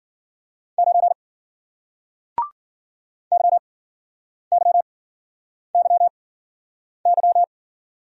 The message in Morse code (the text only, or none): 9 YYYY